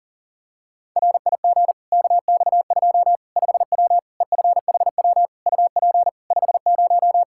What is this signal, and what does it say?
Morse code, heard as a keyed tone, about 37 wpm, 700 Hz. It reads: RIG KX2 5W EFHW UP 50
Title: Practice QSO - Exchange 2 - Info - 40wpm